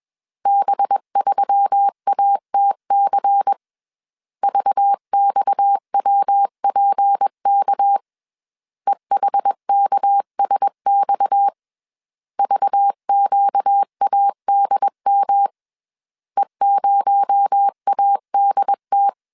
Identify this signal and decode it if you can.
Morse code, heard as a keyed tone, about 21 wpm, 800 Hz. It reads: B3ATC VXWPK EHKSX VQADM E0ADT
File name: CW / DET / 1 6 1500C S B 20wpm 2048